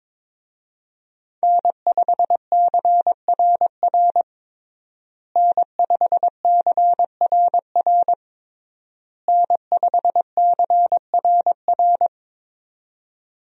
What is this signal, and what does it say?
Morse code, heard as a keyed tone, about 22 wpm, 700 Hz. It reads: N5CRR N5CRR N5CRR
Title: USA Call Signs - 5 Characters - Repeated 3x - 22wpm